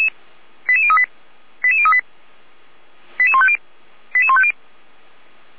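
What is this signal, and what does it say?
Five-tone selective call (ZVEI), heard as quick runs of five tones, tones of about 70 ms. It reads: 88038 88038 88150 88150
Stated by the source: ZVEI five-tone call 89164 received on an SDR, recorded by Feuerwehr Orlamünde.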